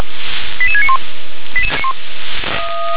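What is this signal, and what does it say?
Five-tone selective call (ZVEI), heard as quick runs of five tones, tones of about 70 ms. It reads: 88691
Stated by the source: ZVEI five-tone call 88557 received via SDR.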